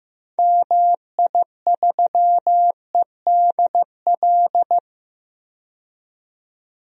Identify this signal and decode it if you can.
Morse code, heard as a keyed tone, about 15 wpm, 700 Hz. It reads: MI3EDL
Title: DXCC-100 Call Signs - 6 Characters - 15wpm